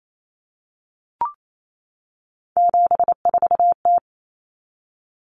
Morse code, heard as a keyed tone, about 28 wpm, 700 Hz. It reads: 74T
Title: Common Cut Numbers - 3-Digit 28wpm